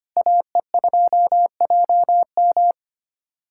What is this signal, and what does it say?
Morse code, heard as a keyed tone, about 25 wpm, 700 Hz. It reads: AE2JM
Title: USA Callsign DE Callsign 25wpm